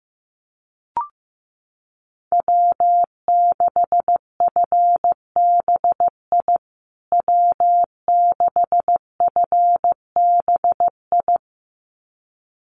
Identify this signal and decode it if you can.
Morse code, heard as a keyed tone, about 15 wpm, 700 Hz. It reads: W6FBI W6FBI
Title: USA Call Signs Repeated 2x 15wpm